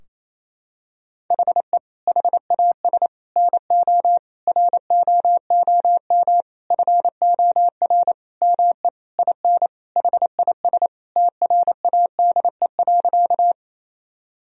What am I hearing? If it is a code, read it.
Morse code, heard as a keyed tone, about 28 wpm, 700 Hz. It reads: HE HAS NO ROOM FOR ME IN HIS TRADE.
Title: Sentences from Top 1000 Words - Review Part 4 - 28wpm